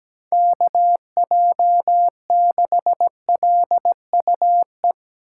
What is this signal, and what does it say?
Morse code, heard as a keyed tone, about 17 wpm, 700 Hz. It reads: KJ6LUE
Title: USA Call Signs - 6 Characters - 8x Character Spacing - 17wpm